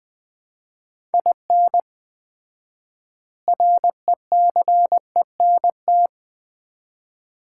Morse code, heard as a keyed tone, about 20 wpm, 700 Hz. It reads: IN RECENT
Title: 4x Word-Spacing - Sets of 3 Words - 20wpm